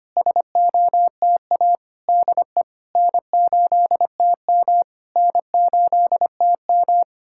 Morse code, heard as a keyed tone, about 25 wpm, 700 Hz. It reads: SOTA DE N8TM N8TM
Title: SOTA QSO Practice 25wpm